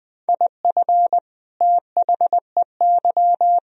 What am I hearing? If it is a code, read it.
Morse code, heard as a keyed tone, about 21 wpm, 700 Hz. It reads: IF THEY
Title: Sentences from Top 1000 Words - Review Part 1 - 20wpm